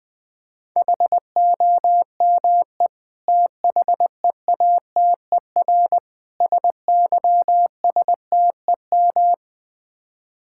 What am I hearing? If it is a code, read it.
Morse code, heard as a keyed tone, about 21 wpm, 700 Hz. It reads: HOME THEATER SYSTEM
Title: Difficult Words - Speed-Racing with Hints - 20wpm